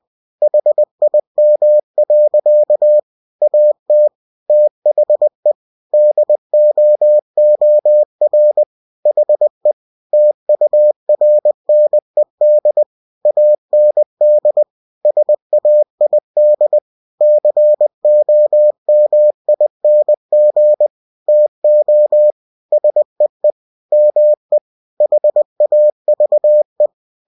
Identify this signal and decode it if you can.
Morse code, heard as a keyed tone, about 20 wpm, 600 Hz. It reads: HIM. AT THE DOOR HE TURNED AND SAID COMING TO SEE ME HAVE
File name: cw-books / winnie-the-pooh / winnie-20wpm-page-0111